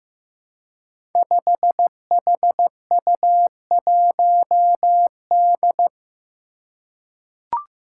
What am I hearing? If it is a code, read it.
Morse code, heard as a keyed tone, about 15 wpm, 700 Hz. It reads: 5HU1D
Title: Five Challenging Letter-Number Combinations 15wpm